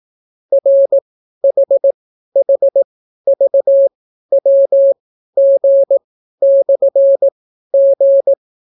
Morse code, heard as a keyed tone, about 18 wpm, spaced as for 12 wpm, 550 Hz. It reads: RHHVWG/G